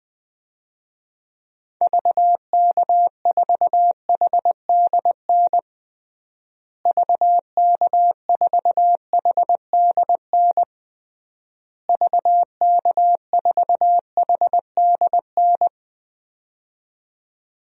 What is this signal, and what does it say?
Morse code, heard as a keyed tone, about 20 wpm, 700 Hz. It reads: VK4HDN VK4HDN VK4HDN
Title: DXCC-100 Call Signs - 6 Characters - Repeated 3x - 20wpm